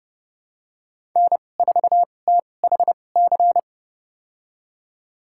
Morse code, heard as a keyed tone, about 30 wpm, spaced as for 22 wpm, 700 Hz. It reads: N4THC
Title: USA Call Signs - 5 Characters - 2x Character Spacing - 30wpm